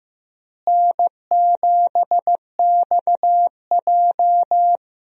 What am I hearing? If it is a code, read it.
Morse code, heard as a keyed tone, about 15 wpm, 700 Hz. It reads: N7XJ
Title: USA Call Signs - 4 Characters - Repeated 3x - 15wpm